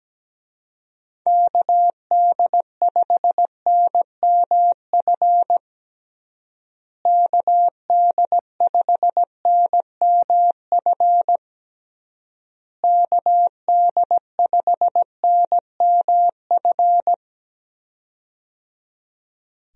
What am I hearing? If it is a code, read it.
Morse code, heard as a keyed tone, about 17 wpm, 700 Hz. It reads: KD5NMF KD5NMF KD5NMF
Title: USA Call Signs - 6 Characters - Repeated 3x - 17wpm